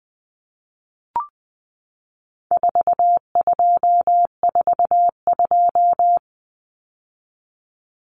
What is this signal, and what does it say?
Morse code, heard as a keyed tone, about 20 wpm, 700 Hz. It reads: 4242
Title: Numbers - 4-Digit 20wpm